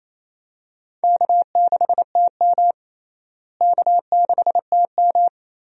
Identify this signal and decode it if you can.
Morse code, heard as a keyed tone, about 28 wpm, 700 Hz. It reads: K6TM K6TM
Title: USA Call Signs - 4 Characters - Repeated 2x - 28wpm